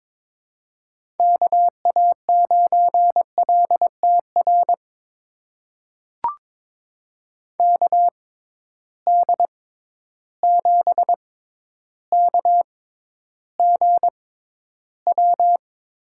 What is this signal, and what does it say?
Morse code, heard as a keyed tone, about 22 wpm, spaced as for 11 wpm, 700 Hz. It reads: KA9LTR KD7KGW
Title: USA Call Signs - 6 Characters - 6x Character Spacing - 22wpm